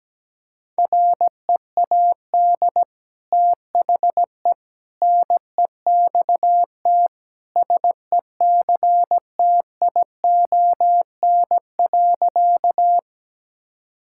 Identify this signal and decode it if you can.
Morse code, heard as a keyed tone, about 17 wpm, 700 Hz. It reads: READ THE NEXT SECTION.
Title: Sentences from Top 1000 Words - Review Part 5 - 17wpm